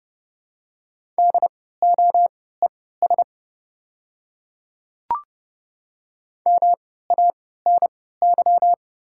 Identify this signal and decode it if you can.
Morse code, heard as a keyed tone, about 30 wpm, spaced as for 17 wpm, 700 Hz. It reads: DOES MANY
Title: 3x Character Spacing - Top 200 Words - 30wpm